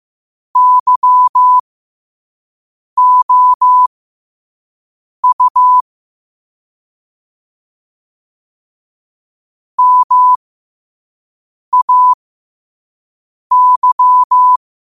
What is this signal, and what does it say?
Morse code, heard as a keyed tone, about 15 wpm, spaced as for 5 wpm, 1000 Hz. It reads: YOU MAY
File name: qotdfiles / 5wpm